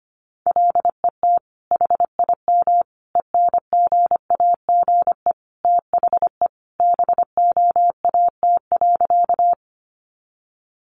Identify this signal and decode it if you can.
Morse code, heard as a keyed tone, about 25 wpm, 700 Hz. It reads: LET HIM ENGAGE THE BOAT.
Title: Sentences from Top 1500 Words 25wpm